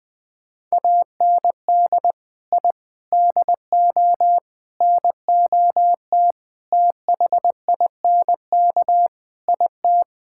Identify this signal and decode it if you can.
Morse code, heard as a keyed tone, about 20 wpm, 700 Hz. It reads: AND I DO NOT THINK IT
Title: Sentences with 5-Word Phrases - Spoken Beginning - 20wpm